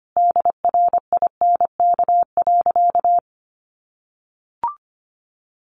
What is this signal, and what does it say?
Morse code, heard as a keyed tone, about 25 wpm, 700 Hz. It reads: DRINK.
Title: Sentences from Top 1500 Words 25wpm